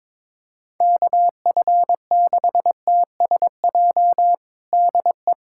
Morse code, heard as a keyed tone, about 22 wpm, 700 Hz. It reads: KF6TSJ DE
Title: USA Callsign DE Callsign - 6 Characters - 22wpm